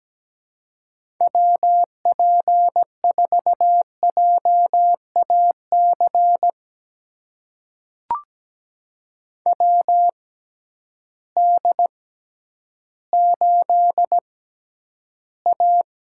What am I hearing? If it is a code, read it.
Morse code, heard as a keyed tone, about 17 wpm, spaced as for 10 wpm, 700 Hz. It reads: WP4JAC WD8A
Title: USA Call Signs - 6 Characters - 6x Character Spacing - 17wpm